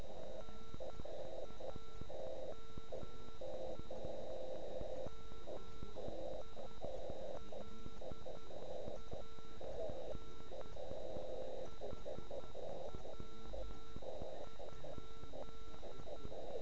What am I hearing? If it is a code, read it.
Morse code, heard as a keyed tone, about 10 wpm, 1340 Hz. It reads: NAME MILAN HW?